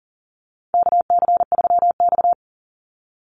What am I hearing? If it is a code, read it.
Morse code, heard as a keyed tone, about 40 wpm, 700 Hz. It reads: KC3X
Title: Call Signs Repeated 2x - 40wpm